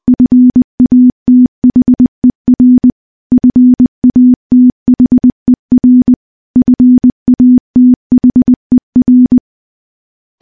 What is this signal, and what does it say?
Morse code, heard as a keyed tone, about 20 wpm, 250 Hz. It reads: FATHER FATHER FATHER
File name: morse-training-20WPM